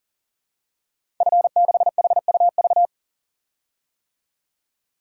Morse code, heard as a keyed tone, about 40 wpm, 700 Hz. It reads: F6HUV
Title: DXCC-100 Call Signs - 5 Characters - 40wpm